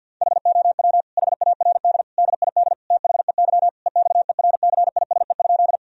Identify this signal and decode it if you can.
Morse code, heard as a keyed tone, about 50 wpm, 700 Hz. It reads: HOW HARD DID THEY EXERCISE?
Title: Sentences from Top 1000 Words - Review Part 4 - 50wpm